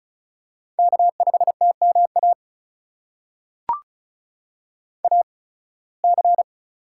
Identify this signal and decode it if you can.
Morse code, heard as a keyed tone, about 36 wpm, spaced as for 23 wpm, 700 Hz. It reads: K5TMA AC
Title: USA Call Signs - 5 Characters - 8x Character Spacing - 35wpm